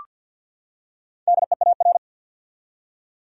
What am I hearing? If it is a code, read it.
Morse code, heard as a keyed tone, about 47 wpm, 700 Hz. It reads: DEAR
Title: Four Letter Words - 50wpm